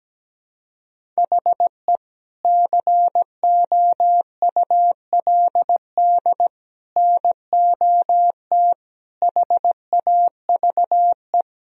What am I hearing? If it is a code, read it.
Morse code, heard as a keyed tone, about 17 wpm, 700 Hz. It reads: HE COULD NOT HAVE
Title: Sentences from Top 1000 Words - Review Part 3 - 17wpm